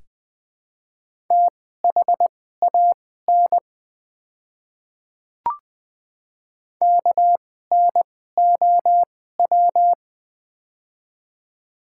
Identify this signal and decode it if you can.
Morse code, heard as a keyed tone, about 20 wpm, spaced as for 14 wpm, 700 Hz. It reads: THAN KNOW
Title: 2x Character Spacing - Top 100 Words - 20wpm